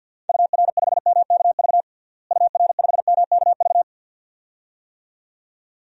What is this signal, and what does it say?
Morse code, heard as a keyed tone, about 52 wpm, 700 Hz. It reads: UR5MKV UR5MKV
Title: DXCC-100 Call Signs - 6 Characters - Repeated 2x - 50wpm